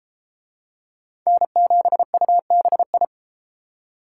Morse code, heard as a keyed tone, about 29 wpm, 700 Hz. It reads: N7UBI